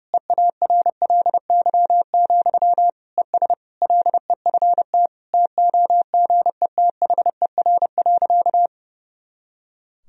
Morse code, heard as a keyed tone, about 30 wpm, 700 Hz. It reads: EARLY, ES LEFT TOGETHER.
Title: Sentences from Top 400 Words and Abbreviations 30wpm